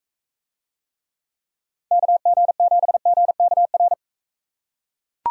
Morse code, heard as a keyed tone, about 41 wpm, 700 Hz. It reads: KG7GKR